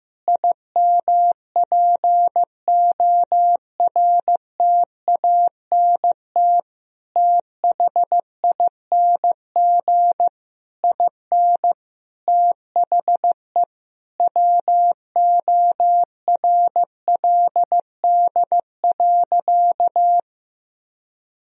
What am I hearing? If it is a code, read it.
Morse code, heard as a keyed tone, about 15 wpm, 700 Hz. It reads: IMPORTANT THING IN THE WORLD.
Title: Sentences from Top 1000 Words - Review Part 1 - 15wpm